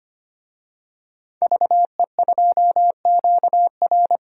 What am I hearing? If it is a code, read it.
Morse code, heard as a keyed tone, about 25 wpm, 700 Hz. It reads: VE2QR
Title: DXCC-100 Call Signs - 5 Characters - Repeated 3x - 25wpm